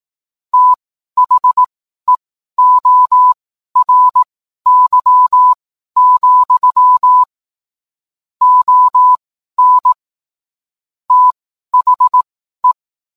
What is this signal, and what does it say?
Morse code, heard as a keyed tone, about 18 wpm, spaced as for 12 wpm, 1000 Hz. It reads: THEORY, ON THE